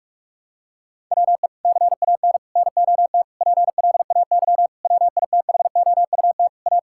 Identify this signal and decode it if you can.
Morse code, heard as a keyed tone, about 42 wpm, 700 Hz. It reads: WE CAN NOT PLAY WITHOUT A
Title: Sentences from Top 1000 Words - Review Part 6 - 45wpm